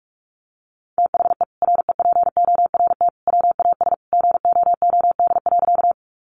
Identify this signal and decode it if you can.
Morse code, heard as a keyed tone, about 44 wpm, 700 Hz. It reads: THE REPORT WAS GOOD.
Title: Sentences from Top 900 Words 45wpm